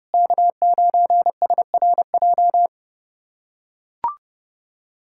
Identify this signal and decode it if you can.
Morse code, heard as a keyed tone, about 32 wpm, 700 Hz. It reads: K9SRJ
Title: USA Callsign DE Callsign - 5 Characters - 30wpm